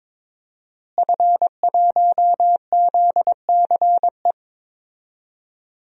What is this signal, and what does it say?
Morse code, heard as a keyed tone, about 22 wpm, 700 Hz. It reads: F1ZCE
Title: DXCC-100 Call Signs - 5 Characters - Repeated 2x - 22wpm